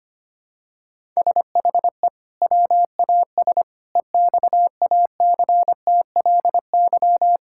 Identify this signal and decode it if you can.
Morse code, heard as a keyed tone, about 25 wpm, 700 Hz. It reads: SHE WAS EXACTLY
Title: Sentences from Top 1000 Words - Review Part 6 - 25wpm